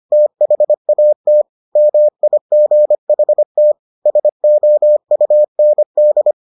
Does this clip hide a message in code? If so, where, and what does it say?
Yes: Morse code from 0.1 to 6.3 s, heard as a keyed tone, about 25 wpm, 600 Hz. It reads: THAT MIGHT SOUND